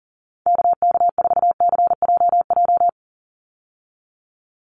Morse code, heard as a keyed tone, about 40 wpm, 700 Hz. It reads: KK4CJJ
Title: USA Call Signs 40wpm